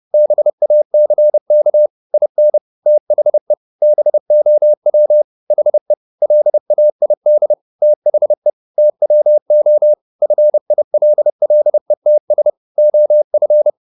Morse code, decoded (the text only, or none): BACK IN THE BOW HE LAID THE TWO FILLETS OF